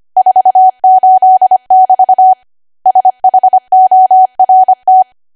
Morse code, heard as a keyed tone, about 25 wpm, 750 Hz. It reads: 48<BT> SHORT
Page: 25 WPM Code Archive